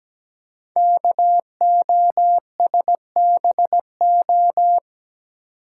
Morse code, heard as a keyed tone, about 17 wpm, 700 Hz. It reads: KOSBO
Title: Code Groups - 5 Letters - 17wpm